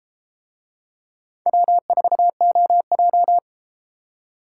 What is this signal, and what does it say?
Morse code, heard as a keyed tone, about 32 wpm, 700 Hz. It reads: W4OJ